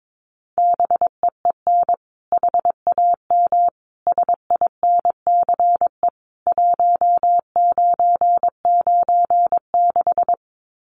Morse code, heard as a keyed tone, about 22 wpm, 700 Hz. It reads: BEEN HAM SINCE 1996
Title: Common QSO Phrases 22wpm